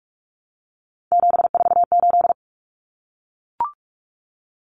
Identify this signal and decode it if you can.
Morse code, heard as a keyed tone, about 44 wpm, 700 Hz. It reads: 748
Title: Common Cut Numbers - 3-Digit 45wpm